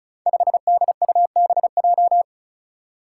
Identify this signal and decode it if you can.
Morse code, heard as a keyed tone, about 36 wpm, 700 Hz. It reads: 5DUBJ
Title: Five Challenging Letter-Number Combinations 35wpm